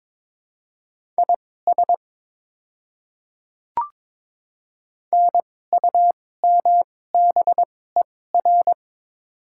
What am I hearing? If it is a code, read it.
Morse code, heard as a keyed tone, about 22 wpm, spaced as for 16 wpm, 700 Hz. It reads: IS NUMBER
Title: 2x Character Spacing - Top 100 Words - 22wpm